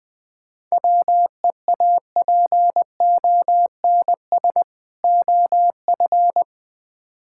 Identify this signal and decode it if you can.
Morse code, heard as a keyed tone, about 21 wpm, 690 Hz. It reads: WEAPONS OF
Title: Build A Phrase - 4-Word Phrases - 20wpm